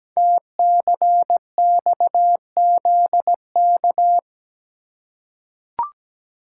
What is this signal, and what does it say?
Morse code, heard as a keyed tone, about 17 wpm, 700 Hz. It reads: TCXZK
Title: Code Groups - 5 Letters - 17wpm